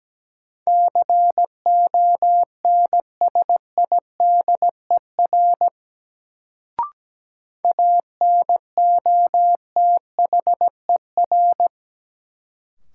Morse code, heard as a keyed tone, about 17 wpm, 700 Hz. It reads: CONSIDER ANOTHER
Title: Sentences from Top 600 Words - New Word Drill - 17wpm